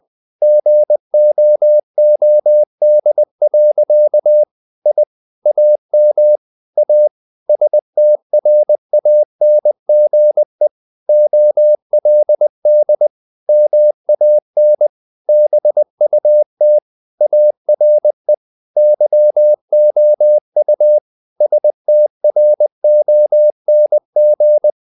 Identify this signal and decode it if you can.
Morse code, heard as a keyed tone, about 20 wpm, 600 Hz. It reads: GOOD. I AM A STRANGE OLD MAN BUT ARE YOU STRONG
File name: the old man and the sea / the old man and the sea part 1 20wpm